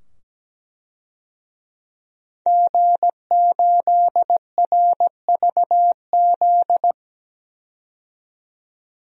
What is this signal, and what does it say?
Morse code, heard as a keyed tone, about 17 wpm, 700 Hz. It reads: G8RVZ
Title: DXCC-100 Call Signs - 5 Characters - 17wpm